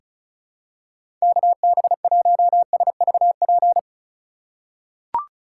Morse code, heard as a keyed tone, about 34 wpm, 700 Hz. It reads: KB1SVP
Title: USA Call Signs - 6 Characters - 3x Character Spacing - 35wpm